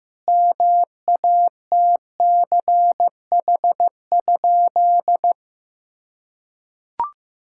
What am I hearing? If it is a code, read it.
Morse code, heard as a keyed tone, about 14 wpm, 700 Hz. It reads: MATCH?
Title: Sentences from Top 1600 Words 15wpm